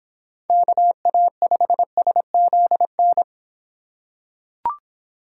Morse code, heard as a keyed tone, about 24 wpm, 700 Hz. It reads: KA5SZN